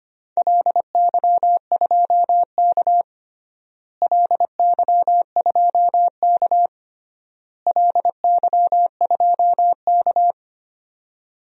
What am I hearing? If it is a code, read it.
Morse code, heard as a keyed tone, about 25 wpm, 700 Hz. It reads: LY2K LY2K LY2K
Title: DXCC-100 Call Signs - 4 Characters - Repeated 3x - 25wpm